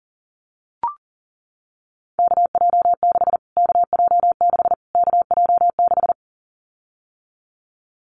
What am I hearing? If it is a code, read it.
Morse code, heard as a keyed tone, about 40 wpm, 700 Hz. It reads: KJ6 KJ6 KJ6
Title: Callsign Prefixes for Top 25 DXCC - Repeated 3x - 40wpm